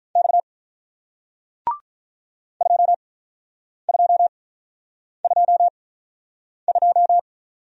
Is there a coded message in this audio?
Yes: Morse code from 0.2 to 7.2 s, heard as a keyed tone, about 48 wpm, spaced as for 9 wpm, 700 Hz. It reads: X 2222